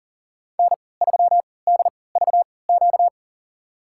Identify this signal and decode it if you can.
Morse code, heard as a keyed tone, about 40 wpm, spaced as for 23 wpm, 700 Hz. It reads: N3DVQ
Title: USA Call Signs - 5 Characters - 3x Character Spacing - 40wpm